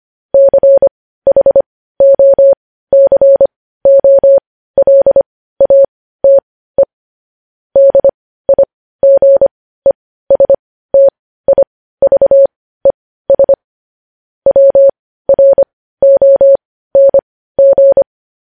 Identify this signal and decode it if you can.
Morse code, heard as a keyed tone, about 25 wpm, spaced as for 15 wpm, 550 Hz. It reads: CHOCOLATE DIGESTIVES WRONG
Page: News Headlines in Morse Code at 15 WPM